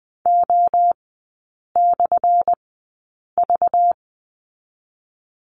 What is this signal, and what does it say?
Morse code, heard as a keyed tone, about 20 wpm, spaced as for 8 wpm, 690 Hz. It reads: O/V
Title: 212 Lesson 34e - Number 8 - Three Characters 20wpm